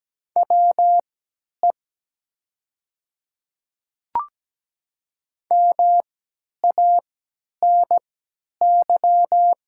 Morse code, heard as a keyed tone, about 17 wpm, spaced as for 10 wpm, 700 Hz. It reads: WE MANY
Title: 3x Character Spacing - Top 100 Words - 17wpm